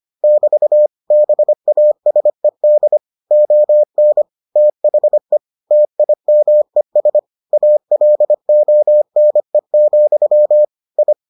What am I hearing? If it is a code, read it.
Morse code, heard as a keyed tone, about 25 wpm, 600 Hz. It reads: <BT> BASED ON THE TIMES ALONE, I